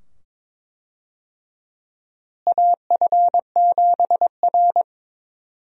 Morse code, heard as a keyed tone, about 21 wpm, 700 Hz. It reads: AF7R
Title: USA Call Signs - 4 Characters - Repeated 2x - 22wpm